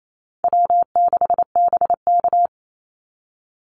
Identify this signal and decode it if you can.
Morse code, heard as a keyed tone, about 28 wpm, 700 Hz. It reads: W6BK
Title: Call Signs - 28wpm